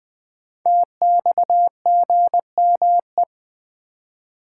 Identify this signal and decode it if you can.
Morse code, heard as a keyed tone, about 20 wpm, 700 Hz. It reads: TXGME
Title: Code Groups - 5 Letter:Numbers - 20wpm